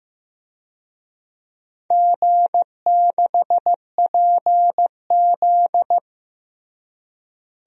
Morse code, heard as a keyed tone, about 15 wpm, 700 Hz. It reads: G6PZ
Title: DXCC-100 Call Signs - 4 Characters - Repeated 3x - 15wpm